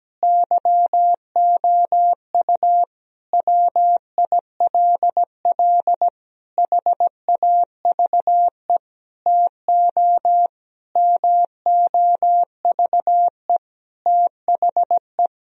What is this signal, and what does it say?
Morse code, heard as a keyed tone, about 17 wpm, 700 Hz. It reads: YOU WILL HAVE TO MOVE THE
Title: Sentences from Top 1000 Words - Review Part 4 - 17wpm